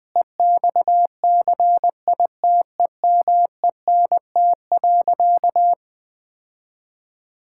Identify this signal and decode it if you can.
Morse code, heard as a keyed tone, about 21 wpm, 700 Hz. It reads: EXCITEMENT.
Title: Difficult Sentences with Hints - 20wpm